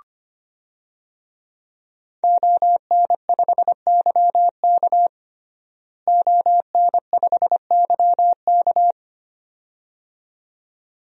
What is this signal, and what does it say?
Morse code, heard as a keyed tone, about 25 wpm, 700 Hz. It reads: ON5YK ON5YK
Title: DXCC-100 Call Signs - 5 Characters - Repeated 2x - 25wpm